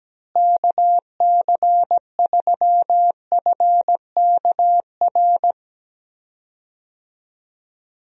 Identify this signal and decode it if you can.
Morse code, heard as a keyed tone, about 17 wpm, 700 Hz. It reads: KC3FKR
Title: USA Call Signs - 6 Characters - Repeated 2x - 17wpm